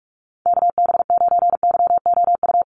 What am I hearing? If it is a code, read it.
Morse code, heard as a keyed tone, about 44 wpm, 700 Hz. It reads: KB9YOU
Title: USA Call Signs Repeated 2x 45wpm